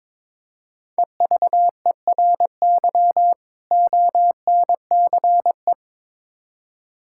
Morse code, heard as a keyed tone, about 21 wpm, 700 Hz. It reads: EVERY ONCE
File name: practice / 22wpm / Build A Phrase - 5-Word Phrases - 22wpm